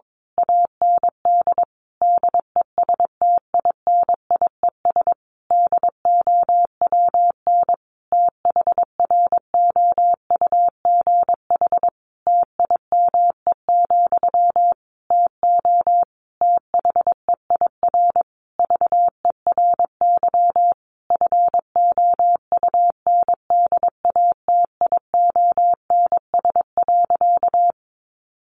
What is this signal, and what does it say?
Morse code, heard as a keyed tone, about 22 wpm, 700 Hz. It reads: AND DESTINIES DOWN THROUGH TIME, TO THEIR VERY FOUNDATIONS.
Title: A Sound Of Thunder 22wpm